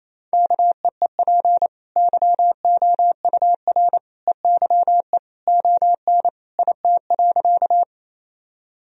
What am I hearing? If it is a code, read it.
Morse code, heard as a keyed tone, about 27 wpm, 700 Hz. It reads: KEEP YOUR EYE ON IT.
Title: Sentences from Top 1000 Words - Review Part 1 - 28wpm